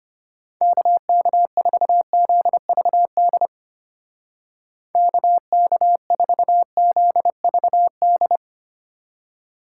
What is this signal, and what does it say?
Morse code, heard as a keyed tone, about 29 wpm, 700 Hz. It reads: KK4ZVD KK4ZVD
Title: Speed Racing - USA 2x3 Call Signs 25wpm